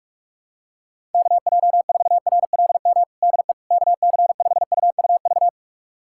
Morse code, heard as a keyed tone, about 46 wpm, 700 Hz. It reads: KJ4RLM DE KC5UUV
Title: USA Callsign DE Callsign - 6 Characters - 45wpm